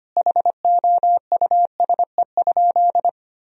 Morse code, heard as a keyed tone, about 25 wpm, 700 Hz. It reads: HOUSE?
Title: Sentences from Top 1000 Words - Review Part 3 - 25wpm